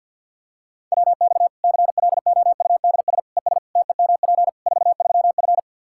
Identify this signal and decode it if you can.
Morse code, heard as a keyed tone, about 48 wpm, 700 Hz. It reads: WX CLOUDS ES TEMP 43F
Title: K1USN SST QSO Practice 50wpm